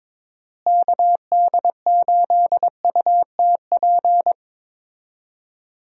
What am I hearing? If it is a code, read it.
Morse code, heard as a keyed tone, about 22 wpm, 700 Hz. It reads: KD8UTP
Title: USA Call Signs - 6 Characters - Repeated 2x - 22wpm